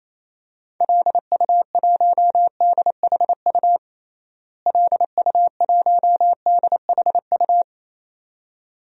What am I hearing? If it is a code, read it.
Morse code, heard as a keyed tone, about 28 wpm, 700 Hz. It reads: LU1DHU LU1DHU
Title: DXCC-100 Call Signs - 6 Characters - Repeated 2x - 28wpm